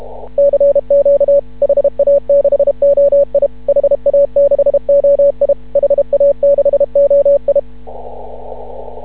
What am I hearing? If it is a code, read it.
Morse code, heard as a keyed tone, about 31 wpm, 560 Hz. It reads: CQ HA6OI HA6OI HA6OI
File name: CW / ha6oi 2